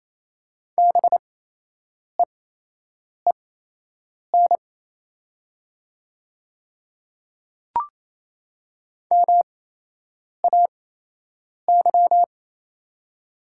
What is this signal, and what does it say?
Morse code, heard as a keyed tone, about 27 wpm, spaced as for 8 wpm, 700 Hz. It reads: BEEN MAY
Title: 8x Character Spacing - Top 200 Words - 28wpm